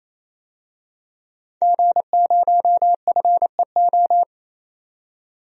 Morse code, heard as a keyed tone, about 28 wpm, 700 Hz. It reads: G0FEO
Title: DXCC-100 Call Signs - 5 Characters - 28wpm